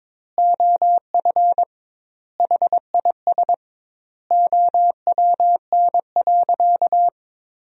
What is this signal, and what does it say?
Morse code, heard as a keyed tone, about 22 wpm, 700 Hz. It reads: OF HIS OWN.
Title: 2x Word-Spacing - Sentences from Top 300 Words - 22wpm